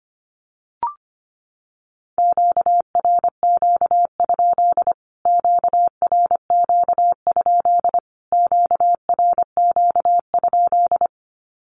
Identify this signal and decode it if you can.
Morse code, heard as a keyed tone, about 25 wpm, 700 Hz. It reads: QRQ? QRQ? QRQ?
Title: Q-Codes Repeated 3x 25wpm